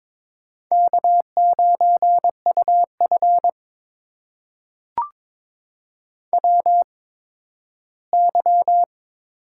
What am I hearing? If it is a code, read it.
Morse code, heard as a keyed tone, about 22 wpm, spaced as for 13 wpm, 700 Hz. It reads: K9UF WY
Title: USA Call Signs - 4 Characters - 8x Character Spacing - 22wpm